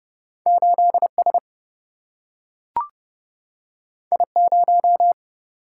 Morse code, heard as a keyed tone, about 30 wpm, 700 Hz. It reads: 8S I0
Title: Single-Letter Single-Number 30wpm